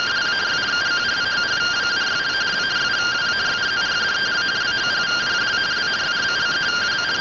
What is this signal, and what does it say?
Two-tone teleprinter signal, about 170 Hz shift.
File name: findings / CT AXIAL -256.gen-41 rtty rx